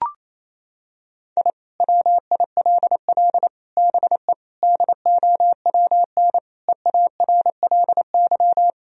Morse code, heard as a keyed tone, about 28 wpm, 700 Hz. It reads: I WILL BE DOWN EARLY
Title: Sentences from Top 1000 Words - Review Part 2 - 28wpm